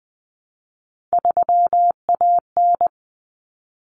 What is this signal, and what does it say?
Morse code, heard as a keyed tone, about 20 wpm, 700 Hz. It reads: 3AN
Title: Common Cut Numbers - 3-Digit 20wpm